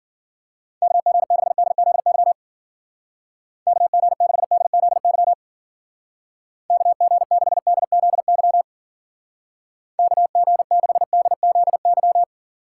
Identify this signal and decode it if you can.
Morse code, heard as a keyed tone, about 50 wpm, 700 Hz. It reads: KG6DZY KG6DZY KG6DZY KG6DZY
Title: Speed Racing - USA 2x3 Call Signs 40wpm